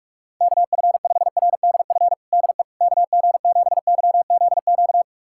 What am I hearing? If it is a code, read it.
Morse code, heard as a keyed tone, about 42 wpm, 700 Hz. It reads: KF5RDF DE KG7YZQ
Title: USA Callsign DE Callsign - 6 Characters - 45wpm